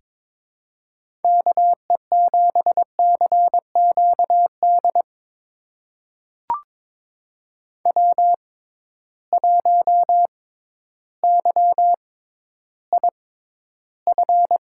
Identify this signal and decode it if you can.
Morse code, heard as a keyed tone, about 22 wpm, spaced as for 12 wpm, 700 Hz. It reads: KE7CQD W1YIF
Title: USA Call Signs - 6x Character Spacing - 22wpm